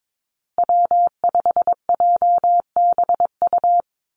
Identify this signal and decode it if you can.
Morse code, heard as a keyed tone, about 22 wpm, 700 Hz. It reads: W5JBU
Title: USA Call Signs Repeated 3x 22wpm